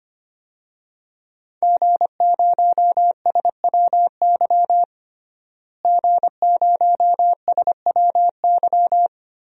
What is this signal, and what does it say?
Morse code, heard as a keyed tone, about 25 wpm, 700 Hz. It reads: G0SWY G0SWY
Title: DXCC-100 Call Signs - 5 Characters - Repeated 2x - 25wpm